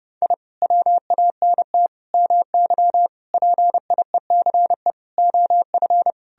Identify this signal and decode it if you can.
Morse code, heard as a keyed tone, about 30 wpm, 700 Hz. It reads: I WANT MY PIECE OF
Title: Sentences from Top 1000 Words - Review Part 2 - 30wpm